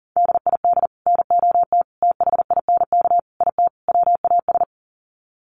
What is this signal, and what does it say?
Morse code, heard as a keyed tone, about 40 wpm, 700 Hz. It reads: DID NOT THINK IT WAS
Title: Sets of 5 Words Encore 40wpm